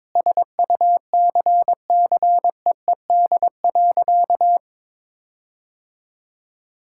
Morse code, heard as a keyed tone, about 22 wpm, 700 Hz. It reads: SUCCEED.
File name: practice / 22wpm / Sentences with 8-Word Phrases - 22wpm